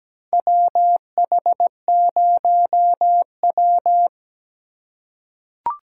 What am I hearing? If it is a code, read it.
Morse code, heard as a keyed tone, about 17 wpm, 700 Hz. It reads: WH0W
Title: USA Call Signs - 4 Characters - 3x Character Spacing - 17wpm